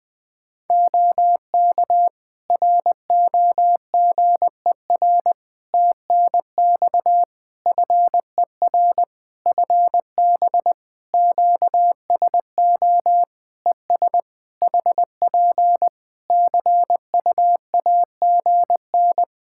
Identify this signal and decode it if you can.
Morse code, heard as a keyed tone, about 20 wpm, 700 Hz. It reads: OK ROGER TNX FER FB QSO ES HP CUAGN
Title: Practice QSO - Exchange 4 - Ending - 20wpm